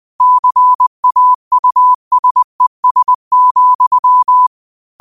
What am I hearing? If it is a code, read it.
Morse code, heard as a keyed tone, about 20 wpm, 1000 Hz. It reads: CAUSES,
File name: qotdfiles / 20wpm